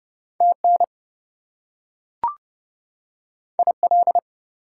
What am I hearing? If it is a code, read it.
Morse code, heard as a keyed tone, about 30 wpm, 700 Hz. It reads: TN IL
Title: US States and Canadian Provinces 30wpm